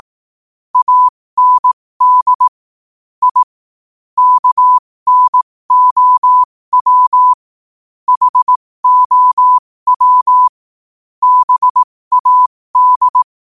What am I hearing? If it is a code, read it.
Morse code, heard as a keyed tone, about 18 wpm, spaced as for 15 wpm, 1000 Hz. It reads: AND I KNOW HOW BAD
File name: qotdfiles / 15wpm